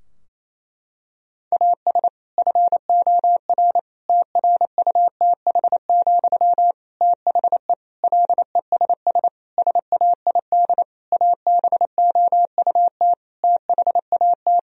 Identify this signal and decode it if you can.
Morse code, heard as a keyed tone, about 28 wpm, 700 Hz. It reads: AS FOR TRUTH, THE LESS SAID ABOUT THAT
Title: Sentences from Top 1000 Words - Review Part 2 - 28wpm